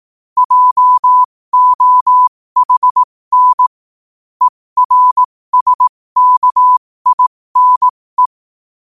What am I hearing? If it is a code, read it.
Morse code, heard as a keyed tone, about 18 wpm, spaced as for 15 wpm, 1000 Hz. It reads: JOHN ERSKINE
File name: qotdfiles / 15wpm